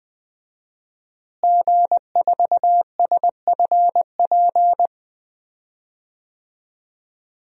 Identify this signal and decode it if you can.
Morse code, heard as a keyed tone, about 20 wpm, 700 Hz. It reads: G4SFP
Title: DXCC-100 Call Signs - 5 Characters - Repeated 2x - 20wpm